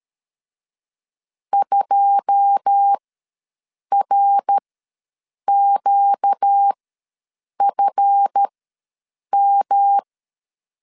Morse code, heard as a keyed tone, about 13 wpm, spaced as for 7 wpm, 800 Hz. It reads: 2RQFM
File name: LEZIONI / 9 2 ws 12wpm 3406